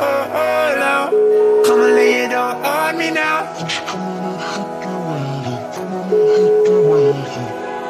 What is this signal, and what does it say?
Ringback tone (Europe, 425 Hz), with speech.